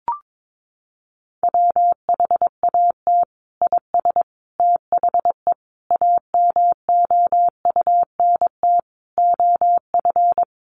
Morse code, heard as a keyed tone, about 22 wpm, 700 Hz. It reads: WHAT IS THE AMOUNT OF
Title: Sentences from Top 1500 Words 22wpm